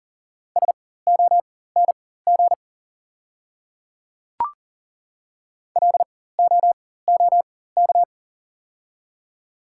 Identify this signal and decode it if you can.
Morse code, heard as a keyed tone, about 40 wpm, spaced as for 19 wpm, 700 Hz. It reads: SONG LOOK
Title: 4x Character Spacing - Top 500 Words - 40wpm